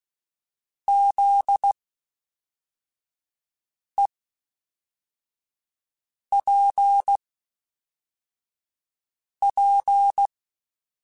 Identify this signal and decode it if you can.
Morse code, heard as a keyed tone, about 16 wpm, spaced as for 4 wpm, 800 Hz. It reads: ZEPP
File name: (spaced) / FromSMs spaced mp3s / Classes@5